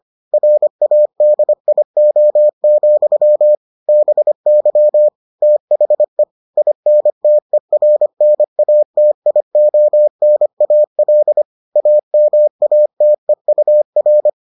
Morse code, heard as a keyed tone, about 25 wpm, 600 Hz. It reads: RADIO, BY THE INTERNATIONAL AMATEUR